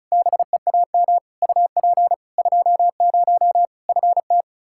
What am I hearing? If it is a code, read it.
Morse code, heard as a keyed tone, about 36 wpm, 700 Hz. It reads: BEAM UP 20 FT
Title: Practice QSO - Exchange 2 - Info - 35wpm